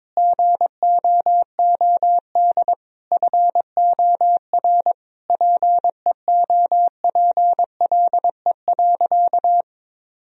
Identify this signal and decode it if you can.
Morse code, heard as a keyed tone, about 22 wpm, 700 Hz. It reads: GOOD FOR PEOPLE.
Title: Sentences from Top 1000 Words - Review Part 3 - 22wpm